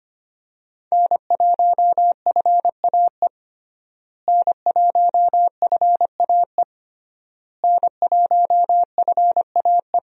Morse code, heard as a keyed tone, about 25 wpm, 700 Hz. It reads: N1FAE N1FAE N1FAE
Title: USA Call Signs - 5 Characters - Repeated 3x - 25wpm